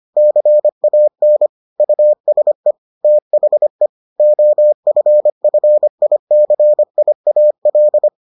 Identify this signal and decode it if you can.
Morse code, heard as a keyed tone, about 25 wpm, 600 Hz. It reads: CAN USE THE OFFICIAL